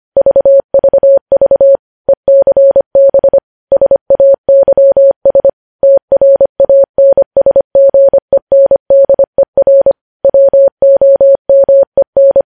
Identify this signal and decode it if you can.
Morse code, heard as a keyed tone, about 25 wpm, 550 Hz. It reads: VVV ECB SAYS TRANSGENDER WOMEN
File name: podcast / News headlines-1x-25x25